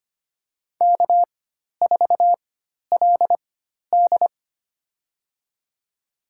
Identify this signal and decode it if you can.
Morse code, heard as a keyed tone, about 25 wpm, spaced as for 12 wpm, 690 Hz. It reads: K4LD
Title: USA Call Signs - 4x Character Spacing - 25wpm